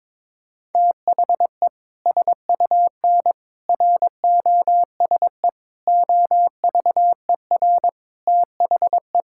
Morse code, heard as a keyed tone, about 22 wpm, 700 Hz. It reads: THE SUN ROSE OVER THE
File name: practice / 22wpm / Sentences with 2-Word Phrases - 22wpm